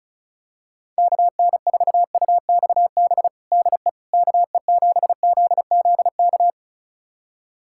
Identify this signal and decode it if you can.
Morse code, heard as a keyed tone, about 34 wpm, 700 Hz. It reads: KN4UXB DE KE7ZZK
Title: USA Callsign DE Callsign - 6 Characters - 35wpm